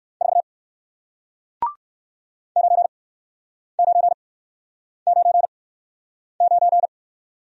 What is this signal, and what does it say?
Morse code, heard as a keyed tone, about 57 wpm, spaced as for 9 wpm, 700 Hz. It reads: 4 9999